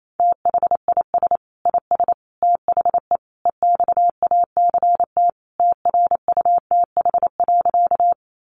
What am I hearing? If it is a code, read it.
Morse code, heard as a keyed tone, about 28 wpm, 700 Hz. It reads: THIS IS THE EXACT TRUTH.
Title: Sentences from Top 1300 Words 28wpm